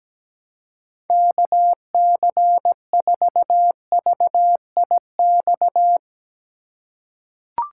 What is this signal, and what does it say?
Morse code, heard as a keyed tone, about 17 wpm, 700 Hz. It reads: KC4VIX
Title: USA Call Signs - 6 Characters - 2x Character Spacing - 17wpm